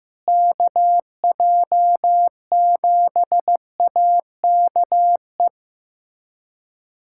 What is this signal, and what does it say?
Morse code, heard as a keyed tone, about 15 wpm, 700 Hz. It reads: KJ7AKE
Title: USA Callsign DE Callsign - 6 Characters - 15wpm